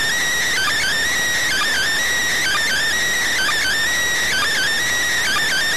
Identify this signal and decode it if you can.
Emergency Alert System (SAME) tones (header bursts), heard as harsh buzzing bursts. It is heard from 0.5 to 4.7 s.